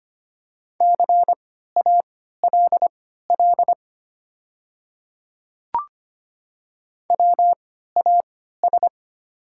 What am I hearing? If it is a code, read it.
Morse code, heard as a keyed tone, about 25 wpm, spaced as for 14 wpm, 690 Hz. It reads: CALL WAS
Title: 3x Character Spacing - Top 200 Words - 25wpm